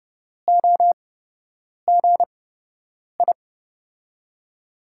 Morse code, heard as a keyed tone, about 31 wpm, spaced as for 8 wpm, 700 Hz. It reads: OGI
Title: USA Call Signs - 5 Characters - 8x Character Spacing - 30wpm